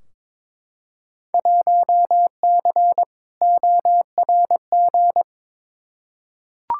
Morse code, heard as a keyed tone, about 22 wpm, 700 Hz. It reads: 1C ORG
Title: Field Day Exchanges - 22wpm